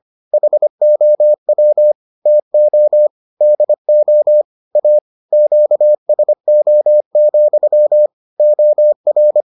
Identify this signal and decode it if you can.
Morse code, heard as a keyed tone, about 25 wpm, 600 Hz. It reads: HOW TO DO A QSO, OR